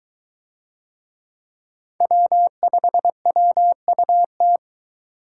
Morse code, heard as a keyed tone, about 23 wpm, 700 Hz. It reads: W5WUT